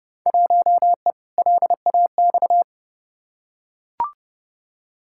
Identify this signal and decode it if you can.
Morse code, heard as a keyed tone, about 31 wpm, 700 Hz. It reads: 1E LAX
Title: Field Day Exchanges - 30wpm